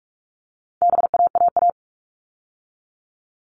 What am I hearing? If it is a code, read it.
Morse code, heard as a keyed tone, about 44 wpm, 700 Hz. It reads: BAAA
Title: All Cut Numbers - 4-Digit 45wpm